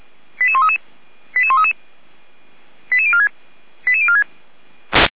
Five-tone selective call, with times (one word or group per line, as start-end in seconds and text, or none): 0.4-0.8 88133
1.4-1.7 88133
2.9-3.3 88046
3.9-4.2 88046